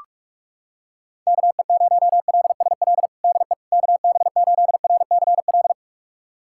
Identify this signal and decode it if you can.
Morse code, heard as a keyed tone, about 41 wpm, 690 Hz. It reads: KE0LSL DE KB8RCL
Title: USA Callsign DE Callsign 45wpm